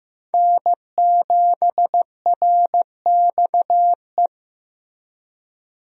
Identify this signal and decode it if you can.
Morse code, heard as a keyed tone, about 15 wpm, 700 Hz. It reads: N7RXE
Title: USA Call Signs - 5 Characters - Repeated 2x - 15wpm